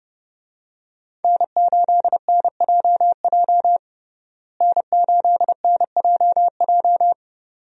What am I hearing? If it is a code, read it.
Morse code, heard as a keyed tone, about 30 wpm, 700 Hz. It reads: N8NJJ N8NJJ
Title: USA Call Signs - 5 Characters - Repeated 2x - 30wpm